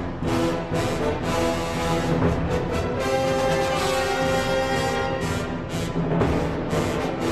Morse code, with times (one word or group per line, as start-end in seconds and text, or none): none